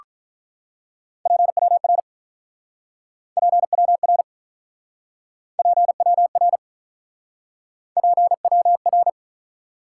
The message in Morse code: PWR PWR PWR PWR

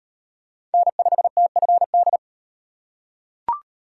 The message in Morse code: N5TFD

